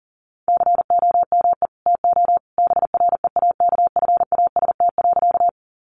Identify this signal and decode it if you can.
Morse code, heard as a keyed tone, about 37 wpm, 700 Hz. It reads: COME TO BREAKFAST.
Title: Sentences from Top 1500 Words 40wpm